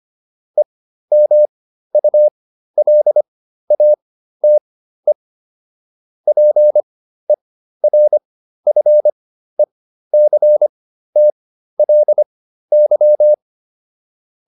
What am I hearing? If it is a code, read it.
Morse code, heard as a keyed tone, about 25 wpm, spaced as for 13 wpm, 600 Hz. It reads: EMULATE PERFECTLY